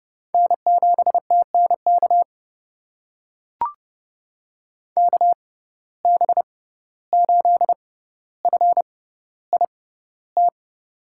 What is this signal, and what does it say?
Morse code, heard as a keyed tone, about 32 wpm, spaced as for 15 wpm, 700 Hz. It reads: N7TNK KB8FIT
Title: USA Call Signs - 6x Character Spacing - 30wpm